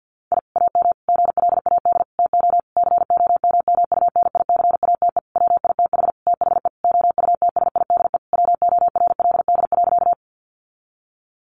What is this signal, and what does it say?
Morse code, heard as a keyed tone, about 50 wpm, 700 Hz. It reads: I AM GLAD TO COMMUNICATE WITH THE OUTSIDE WORLD.
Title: Sentences from Top 1300 Words 50wpm